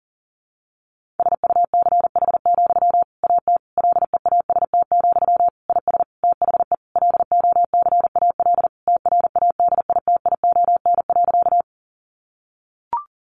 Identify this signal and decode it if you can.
Morse code, heard as a keyed tone, about 40 wpm, 700 Hz. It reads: SUCH, AT LEAST, IS THE LOCAL TRADITION.
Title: Sentences from Top 1500 Words 40wpm